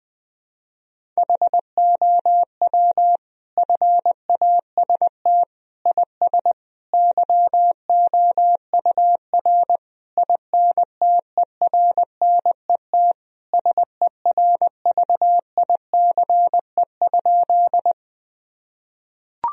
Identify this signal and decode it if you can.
Morse code, heard as a keyed tone, about 21 wpm, 700 Hz. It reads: HOW FAST IS YOUR INTERNET SERVICE?
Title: Sentences from Top 1000 Words - Review Part 4 - 20wpm